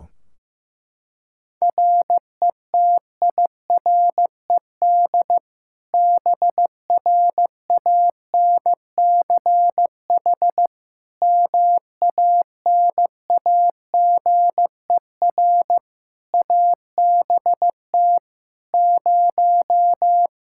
Morse code, heard as a keyed tone, about 15 wpm, 700 Hz. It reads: RETIRED BRANCH MANAGER ABT 0
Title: Practice QSO - Exchange 3 - More Info - 15wpm